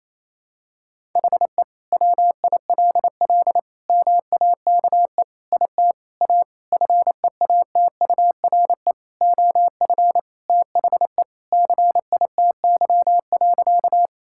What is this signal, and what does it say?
Morse code, heard as a keyed tone, about 28 wpm, 700 Hz. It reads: HE WILL MAKE IT A FEATURE OF THE CITY.
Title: Sentences from Top 1000 Words - Review Part 5 - 28wpm